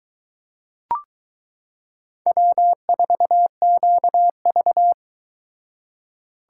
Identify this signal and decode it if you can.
Morse code, heard as a keyed tone, about 23 wpm, 700 Hz. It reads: W4QV